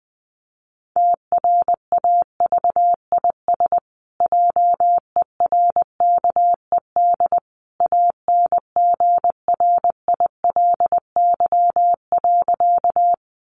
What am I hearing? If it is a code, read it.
Morse code, heard as a keyed tone, about 20 wpm, 700 Hz. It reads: TRAVIS JERKED ANGRILY.
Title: A Sound Of Thunder 20wpm